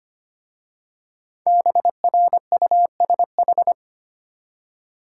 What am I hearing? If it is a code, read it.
Morse code, heard as a keyed tone, about 25 wpm, 700 Hz. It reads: BRUSH
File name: practice / 25wpm / Easy Words with Hints - 25wpm